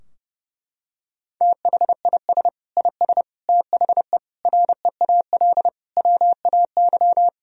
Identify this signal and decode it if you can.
Morse code, heard as a keyed tone, about 29 wpm, 700 Hz. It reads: THIS IS THE REAL WAY
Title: Sentences from Top 1000 Words - Review Part 2 - 30wpm